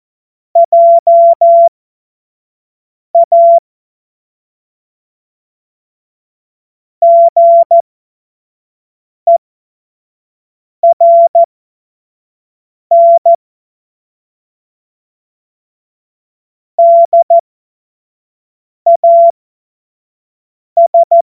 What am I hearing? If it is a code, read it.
Morse code, heard as a keyed tone, about 14 wpm, spaced as for 5 wpm, 680 Hz. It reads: JA GERN DAS
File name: Witz-08-Juli-2025-25-F70